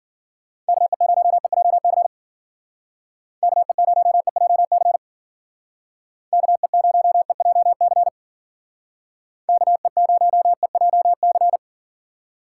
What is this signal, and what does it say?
Morse code, heard as a keyed tone, about 52 wpm, 690 Hz. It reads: KE0EJC KE0EJC KE0EJC KE0EJC